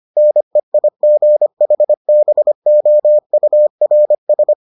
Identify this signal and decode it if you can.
Morse code, heard as a keyed tone, about 25 wpm, 600 Hz. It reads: NEIGHBOURS